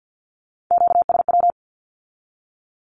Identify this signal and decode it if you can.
Morse code, heard as a keyed tone, about 50 wpm, 700 Hz. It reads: QSW
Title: Q-Codes with Spoken Meaning - 50wpm